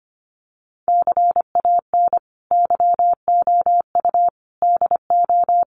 Morse code, heard as a keyed tone, about 25 wpm, 700 Hz. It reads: CAN YOU DO
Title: Sentences from Top 100 Words 25wpm